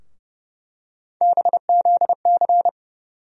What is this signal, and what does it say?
Morse code, heard as a keyed tone, about 30 wpm, 700 Hz. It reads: BZC
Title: USA Call Sign Postfixes - 3 Letters 30wpm